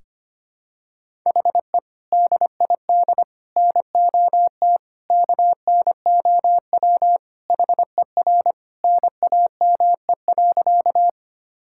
Morse code, heard as a keyed tone, about 25 wpm, 700 Hz. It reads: HE DID NOT KNOW HER NAME.
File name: practice / 25wpm / Sentences with 4-Word Phrases - 25wpm